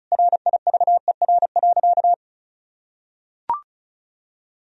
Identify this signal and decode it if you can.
Morse code, heard as a keyed tone, about 34 wpm, 700 Hz. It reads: RIVER.
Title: Sentences from Top 1000 Words - Review Part 1 - 35wpm